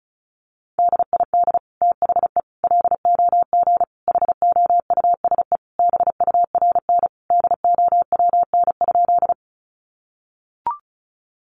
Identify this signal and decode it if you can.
Morse code, heard as a keyed tone, about 34 wpm, 700 Hz. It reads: DID THE LOG HOUSE BURN DOWN?
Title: Sentences from Top 1400 Words 35wpm